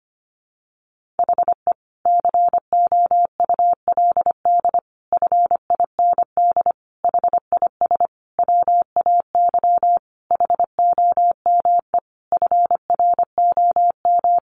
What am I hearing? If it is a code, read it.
Morse code, heard as a keyed tone, about 25 wpm, 700 Hz. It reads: HE COULD FIND HIS WAY HOME FROM
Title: Sentences from Top 1400 Words 25wpm